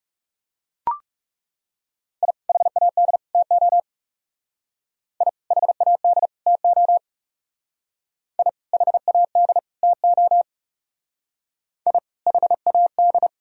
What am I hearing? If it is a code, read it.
Morse code, heard as a keyed tone, about 36 wpm, 700 Hz. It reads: I HAD TO I HAD TO I HAD TO I HAD
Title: Speed-Racing - Sets of 3 Words 30wpm